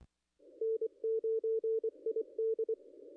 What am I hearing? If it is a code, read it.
Morse code, heard as a keyed tone, about 25 wpm, 440 Hz. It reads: N9ID
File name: contests / ss-cw-k5tr-op-n5rz -0200